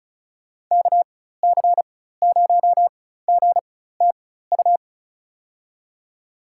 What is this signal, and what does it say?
Morse code, heard as a keyed tone, about 34 wpm, spaced as for 16 wpm, 700 Hz. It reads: KC0GTU
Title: USA Call Signs - 6 Characters - 4x Character Spacing - 35wpm